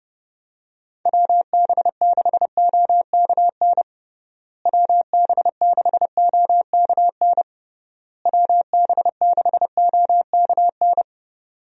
Morse code, heard as a keyed tone, about 30 wpm, 700 Hz. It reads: WB6OKN WB6OKN WB6OKN
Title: USA Call Signs - 6 Characters - Repeated 3x - 30wpm